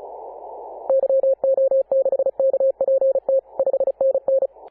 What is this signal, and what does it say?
Morse code, heard as a keyed tone, about 34 wpm, 540 Hz. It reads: YO6KPT 5NN
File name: Apr/29 2042z JD1BLY 14.011MHz CW